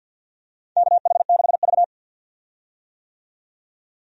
Morse code, heard as a keyed tone, about 47 wpm, 700 Hz. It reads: KH6V
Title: USA Call Signs - 4 Characters - 50wpm